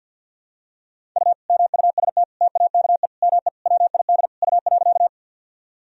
Morse code, heard as a keyed tone, about 51 wpm, 700 Hz. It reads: U MUST TAKE ME WID U.
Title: Sentences from Top 200 Words and Abbreviations 50wpm